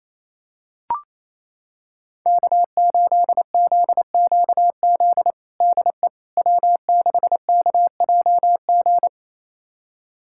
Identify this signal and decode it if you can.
Morse code, heard as a keyed tone, about 27 wpm, 700 Hz. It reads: K8ZQZ DE W6KJG
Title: USA Callsign DE Callsign - 5 Characters - 28wpm